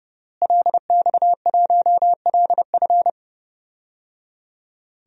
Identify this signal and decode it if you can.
Morse code, heard as a keyed tone, about 30 wpm, 700 Hz. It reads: LX1LF
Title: DXCC-100 Call Signs - 5 Characters - Repeated 3x - 30wpm